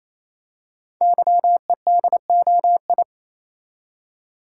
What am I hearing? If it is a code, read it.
Morse code, heard as a keyed tone, about 29 wpm, 700 Hz. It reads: YEDOI
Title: Code Groups - 5 Letters - 28wpm